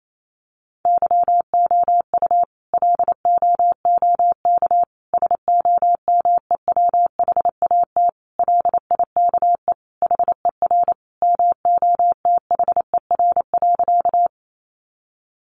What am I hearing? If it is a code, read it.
Morse code, heard as a keyed tone, about 28 wpm, 700 Hz. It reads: YOU LOOK SOMEWHAT LIKE HER MOTHER.
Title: Sentences from Top 1300 Words 28wpm